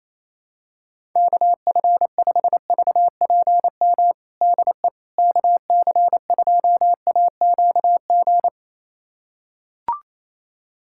Morse code, heard as a keyed tone, about 28 wpm, 700 Hz. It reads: KF5VPM DE KC2AQG
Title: USA Callsign DE Callsign - 6 Characters - 28wpm